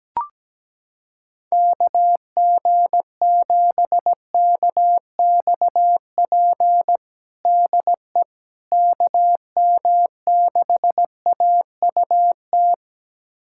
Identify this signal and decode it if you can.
Morse code, heard as a keyed tone, about 17 wpm, 700 Hz. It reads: KG7KXP DE KM6AUT
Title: USA Callsign DE Callsign - 6 Characters - 17wpm